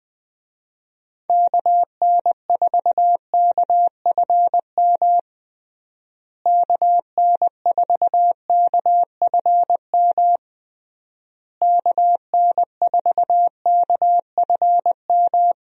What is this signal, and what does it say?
Morse code, heard as a keyed tone, about 20 wpm, 700 Hz. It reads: KN4KFM KN4KFM KN4KFM
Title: USA Call Signs - 6 Characters - Repeated 3x - 20wpm